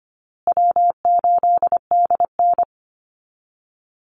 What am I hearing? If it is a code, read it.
Morse code, heard as a keyed tone, about 25 wpm, 700 Hz. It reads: W8DN
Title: Call Signs Repeated 2x - 25wpm